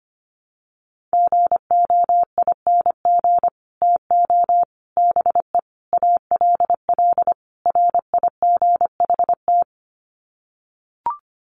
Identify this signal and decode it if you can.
Morse code, heard as a keyed tone, about 25 wpm, 700 Hz. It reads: GOING TO BE ALL RIGHT
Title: Sets of 5 Words Encore2 25wpm